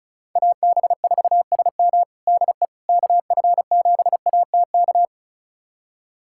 Morse code, heard as a keyed tone, about 36 wpm, 700 Hz. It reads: AB4SM DE KF7ATK
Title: USA Callsign DE Callsign 35wpm